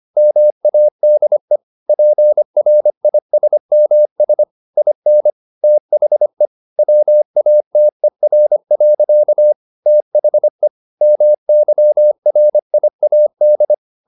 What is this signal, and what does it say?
Morse code, heard as a keyed tone, about 25 wpm, 600 Hz. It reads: MADE PRISMS IN THE WATER. THE MYRIAD
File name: the old man and the sea part 1 25wpm